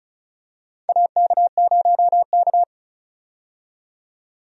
Morse code, heard as a keyed tone, about 36 wpm, 700 Hz. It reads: AK0K
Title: USA Call Signs - 4 Characters - 35wpm